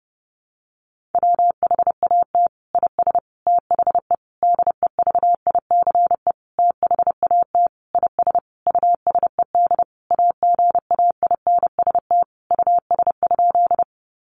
Morse code, heard as a keyed tone, about 30 wpm, 700 Hz. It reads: WHAT IS THE DEVICE THAT IS USED AGAINST US?
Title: Sentences from Top 1500 Words 30wpm